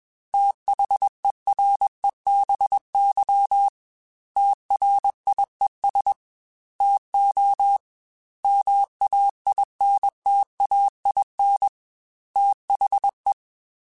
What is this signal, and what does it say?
Morse code, heard as a keyed tone, about 21 wpm, 800 Hz. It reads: THEREBY TRIES TO MAINTAIN THE
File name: Technical Subjects / Technical Subjects-mp3s / XC @20